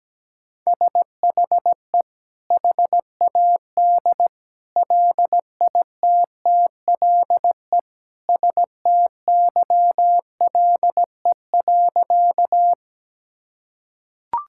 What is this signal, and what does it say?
Morse code, heard as a keyed tone, about 17 wpm, 700 Hz. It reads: SHE HAD LITTLE STYLE.
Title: Sentences from Top 1000 Words - Review Part 5 - 17wpm